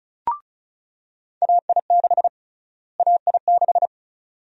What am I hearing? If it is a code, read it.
Morse code, heard as a keyed tone, about 36 wpm, 700 Hz. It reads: AI6 AI6
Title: USA Call Sign Prefixes - Repeated 2x - 35wpm